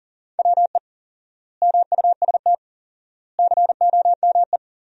Morse code, heard as a keyed tone, about 37 wpm, 700 Hz. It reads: WE MUST COME
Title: 4x Word-Spacing - Sentences from Top 200 Words - 40wpm